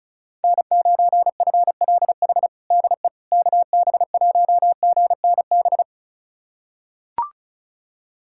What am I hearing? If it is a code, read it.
Morse code, heard as a keyed tone, about 34 wpm, 700 Hz. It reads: N9FLH DE KB1GNB
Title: USA Callsign DE Callsign 35wpm